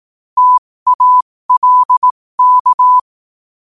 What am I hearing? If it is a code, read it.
Morse code, heard as a keyed tone, about 18 wpm, spaced as for 15 wpm, 1000 Hz. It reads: TALK
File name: qotdfiles / 15wpm